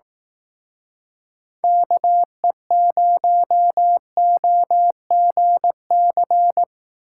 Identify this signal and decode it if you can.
Morse code, heard as a keyed tone, about 18 wpm, 700 Hz. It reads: KE0OGC
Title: Speed Racing - USA 2x3 Call Signs 15wpm